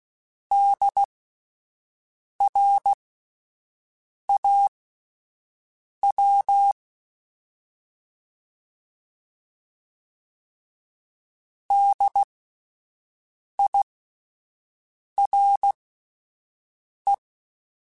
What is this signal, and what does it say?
Morse code, heard as a keyed tone, about 16 wpm, spaced as for 5 wpm, 800 Hz. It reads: DRAW DIRE